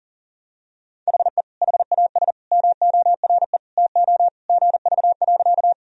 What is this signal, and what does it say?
Morse code, heard as a keyed tone, about 40 wpm, 700 Hz. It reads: HE HAS MORE TO GV.
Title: Sentences from Top 200 Words and Abbreviations 40wpm